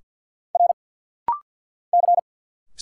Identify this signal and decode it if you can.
Morse code, heard as a keyed tone, about 50 wpm, 700 Hz. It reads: R C
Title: Single Letters - Rapid-Fire 50wpm